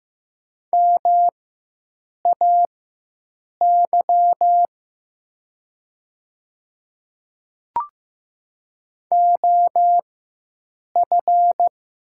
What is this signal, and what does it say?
Morse code, heard as a keyed tone, about 15 wpm, spaced as for 7 wpm, 700 Hz. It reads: MAY OF
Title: 4x Character Spacing - Top 100 Words - 15wpm